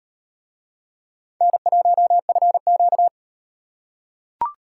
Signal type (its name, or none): Morse code